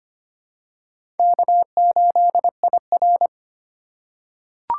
Morse code, heard as a keyed tone, about 25 wpm, 700 Hz. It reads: K8IR
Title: USA Call Signs - 4 Characters - 4x Character Spacing - 25wpm